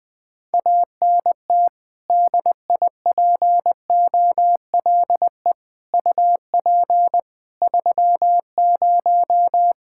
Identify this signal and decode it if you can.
Morse code, heard as a keyed tone, about 20 wpm, 700 Hz. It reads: ANT DIPOLE UP 30
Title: K1USN SST QSO Practice 20wpm